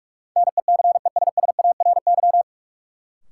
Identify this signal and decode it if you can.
Morse code, heard as a keyed tone, about 46 wpm, 700 Hz. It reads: NECESSARY
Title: Sentences from Top 800 Words - New Word Drill - 45wpm